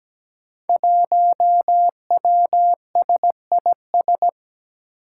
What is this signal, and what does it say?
Morse code, heard as a keyed tone, about 17 wpm, 700 Hz. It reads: 1WSIS
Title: Five Challenging Letter-Number Combinations 17wpm